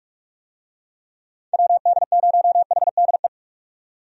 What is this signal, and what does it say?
Morse code, heard as a keyed tone, about 45 wpm, 700 Hz. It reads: WD0HDE
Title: USA Call Signs - 6 Characters - 45wpm